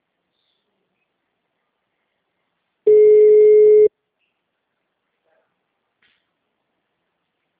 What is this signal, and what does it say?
Ringback tone (Europe, 425 Hz).